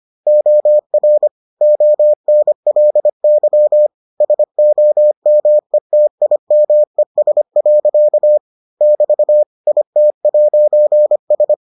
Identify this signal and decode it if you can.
Morse code, heard as a keyed tone, about 25 wpm, 600 Hz. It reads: OR ONLY SOMETIMES. <BT> IT'S